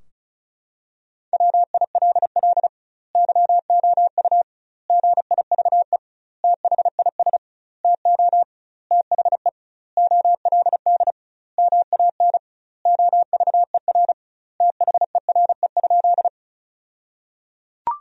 Morse code, heard as a keyed tone, about 36 wpm, 700 Hz. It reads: WILL YOU GIVE THIS TO THE OLD MAN OVER THERE?
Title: 2x Word-Spacing - Sentences from Top 200 Words - 35wpm